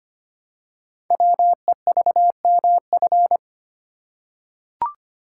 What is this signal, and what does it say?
Morse code, heard as a keyed tone, about 25 wpm, 700 Hz. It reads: WEVMF
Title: Code Groups - 5 Letters - 25wpm